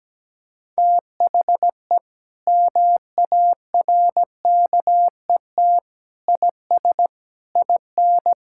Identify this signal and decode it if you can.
Morse code, heard as a keyed tone, about 17 wpm, 700 Hz. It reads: THE MARKET IS IN
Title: Sentences from Top 1000 Words - Review Part 6 - 17wpm